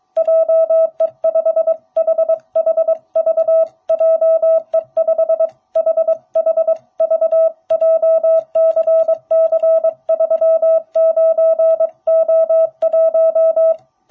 Morse code, heard as a keyed tone, about 23 wpm, 660 Hz. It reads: JE5HHV JE5HHV JCC39O1